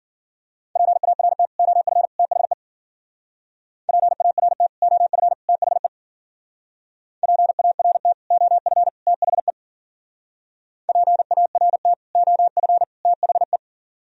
Morse code, heard as a keyed tone, about 55 wpm, 700 Hz. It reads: PART OF THE PART OF THE PART OF THE PART OF THE